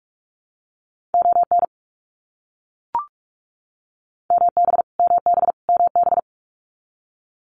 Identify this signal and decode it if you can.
Morse code, heard as a keyed tone, about 44 wpm, 700 Hz. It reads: ON MB MB MB
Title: Canadian Provinces and Territories - Repeated 3x - 45wpm